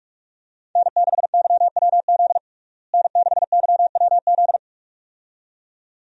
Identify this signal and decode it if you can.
Morse code, heard as a keyed tone, about 47 wpm, 700 Hz. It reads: N6YWZ N6YWZ
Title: USA Call Signs - 5 Characters - Repeated 2x - 45wpm